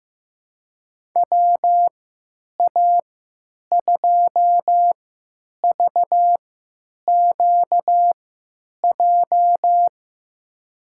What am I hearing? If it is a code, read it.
Morse code, heard as a keyed tone, about 16 wpm, spaced as for 9 wpm, 700 Hz. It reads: WA2VQJ
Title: USA Call Signs - 6 Characters - 3x Character Spacing - 15wpm